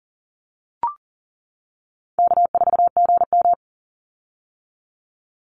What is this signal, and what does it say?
Morse code, heard as a keyed tone, about 40 wpm, 700 Hz. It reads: K4GM
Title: Call Signs - 40wpm